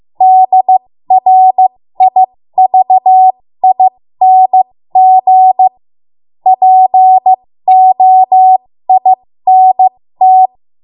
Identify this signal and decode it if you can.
Morse code, heard as a keyed tone, about 15 wpm, spaced as for 13 wpm, 750 Hz. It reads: DRIVING POINT
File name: Morse / 13WPM